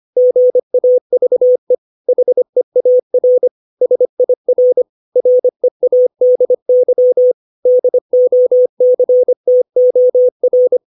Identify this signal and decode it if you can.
Morse code, heard as a keyed tone, about 25 wpm, 500 Hz. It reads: GAVE HEAR SIR READY DOCTOR
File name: 632 words / 632 Words 25WPM